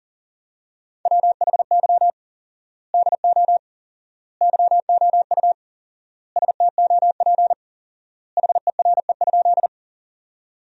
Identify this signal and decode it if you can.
Morse code, heard as a keyed tone, about 40 wpm, 700 Hz. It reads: WHY DO YOU STOP HERE?
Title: 4x Word-Spacing - Sentences from Top 300 Words - 40wpm